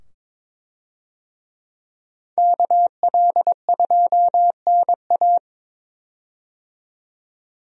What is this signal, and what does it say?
Morse code, heard as a keyed tone, about 22 wpm, 700 Hz. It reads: KL2NA
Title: USA Call Signs - 5 Characters - 22wpm